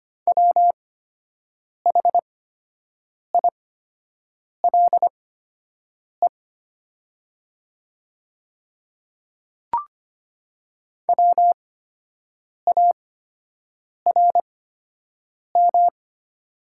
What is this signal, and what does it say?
Morse code, heard as a keyed tone, about 25 wpm, spaced as for 7 wpm, 700 Hz. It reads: WHILE WARM
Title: 8x Character Spacing - Top 500 Words - 25wpm